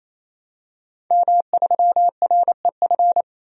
Morse code, heard as a keyed tone, about 28 wpm, 700 Hz. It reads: M3REF
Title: DXCC-100 Call Signs - 5 Characters - 28wpm